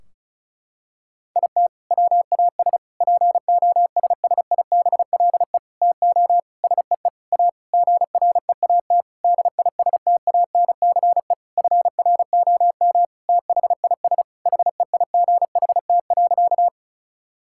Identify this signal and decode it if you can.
Morse code, heard as a keyed tone, about 34 wpm, 700 Hz. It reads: IT WAS POSSIBLE TO SEE A GREAT DISTANCE FROM THIS HEIGHT.
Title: Sentences from Top 1600 Words 35wpm